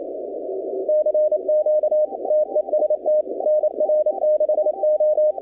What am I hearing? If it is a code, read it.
Morse code, heard as a keyed tone, about 27 wpm, 610 Hz, with radio noise. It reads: CQ TEST NR6O
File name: contests / 2019-cw-wpx-kt5j-k5pi- -2330 0-20